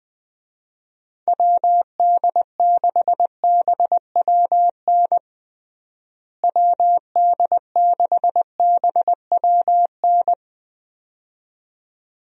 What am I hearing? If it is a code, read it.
Morse code, heard as a keyed tone, about 20 wpm, 700 Hz. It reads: WD6BWN WD6BWN
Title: USA Call Signs - 6 Characters - Repeated 2x - 20wpm